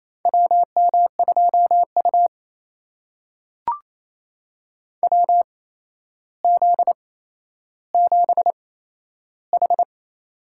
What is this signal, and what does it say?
Morse code, heard as a keyed tone, about 28 wpm, spaced as for 12 wpm, 700 Hz. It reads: WM2U WZ7H
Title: USA Call Signs - 4 Characters - 8x Character Spacing - 28wpm